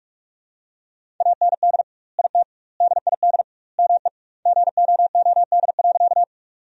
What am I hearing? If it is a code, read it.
Morse code, heard as a keyed tone, about 42 wpm, 700 Hz. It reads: AND IT DID ME GOOD.
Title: 2x Word-Spacing - Sentences from Top 200 Words - 45wpm